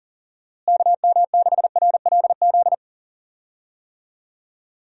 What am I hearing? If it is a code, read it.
Morse code, heard as a keyed tone, about 40 wpm, 700 Hz. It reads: KM6RLZ
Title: USA Call Signs - 6 Characters - Repeated 3x - 40wpm